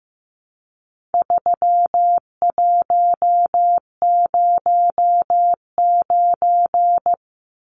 Morse code, heard as a keyed tone, about 15 wpm, 700 Hz. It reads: 3109
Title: Numbers - 4-Digit 15wpm